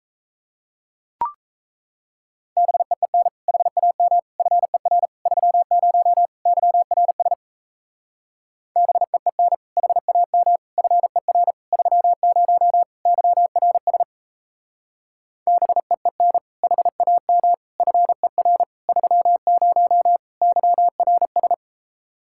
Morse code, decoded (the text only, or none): BEEN HAM FER 30 YRS BEEN HAM FER 30 YRS BEEN HAM FER 30 YRS